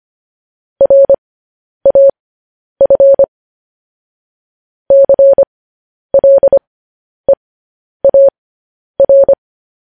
Morse code, heard as a keyed tone, about 25 wpm, spaced as for 10 wpm, 550 Hz. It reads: RAF CLEAR
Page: news headlines in morse code at 10 wpm